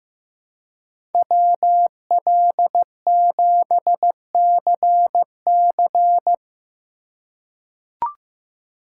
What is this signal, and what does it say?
Morse code, heard as a keyed tone, about 15 wpm, 700 Hz. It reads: WL7CC
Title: USA Call Signs - 5 Characters - 4x Character Spacing - 15wpm